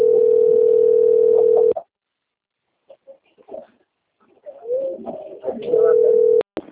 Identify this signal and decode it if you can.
Ringback tone (North America), 440 + 480 Hz.
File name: rg-1600-unknown- - - .46439